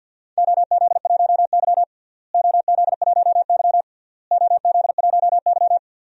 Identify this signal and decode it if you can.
Morse code, heard as a keyed tone, about 52 wpm, 700 Hz. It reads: OZ1Y OZ1Y OZ1Y
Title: DXCC-100 Call Signs - 4 Characters - Repeated 3x - 50wpm